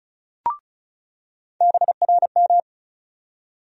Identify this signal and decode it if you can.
Morse code, heard as a keyed tone, about 36 wpm, 700 Hz. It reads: BRM
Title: USA Call Sign Postfixes - 3 Letters 35wpm